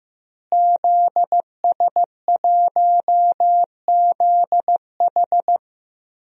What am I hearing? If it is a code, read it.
Morse code, heard as a keyed tone, about 15 wpm, 700 Hz. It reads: ZS1ZH
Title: DXCC-100 Call Signs - 5 Characters - Repeated 2x - 15wpm